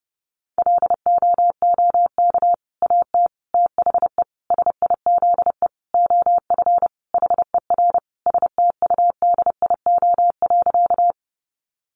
Morse code, heard as a keyed tone, about 30 wpm, 700 Hz. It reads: LOOK AT THE SIZE OF HER STUDIO.
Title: Sentences from Top 1200 Words 30wpm